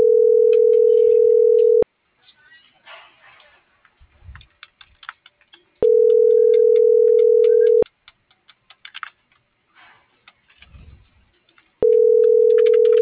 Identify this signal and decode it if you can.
Ringback tone (North America), 440 + 480 Hz.